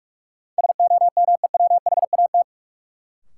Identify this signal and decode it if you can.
Morse code, heard as a keyed tone, about 47 wpm, 700 Hz. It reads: SOMEWHAT
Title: Sentences from Top 1000 Words - Review Part 1 - New Word Drill - 45wpm